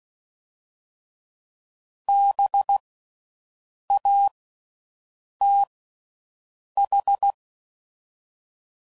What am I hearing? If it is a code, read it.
Morse code, heard as a keyed tone, about 16 wpm, 800 Hz. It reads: B A T H
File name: Suffixes / General / Houses